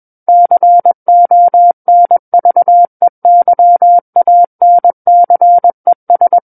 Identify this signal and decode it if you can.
Morse code, heard as a keyed tone, about 21 wpm, 700 Hz. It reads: CONVEYANCES
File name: Book One / 12-Chapter 12 - 21 WPM